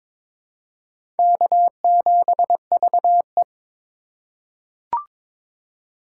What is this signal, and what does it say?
Morse code, heard as a keyed tone, about 22 wpm, 700 Hz. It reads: K7VE
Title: USA Call Signs - 4 Characters - 3x Character Spacing - 22wpm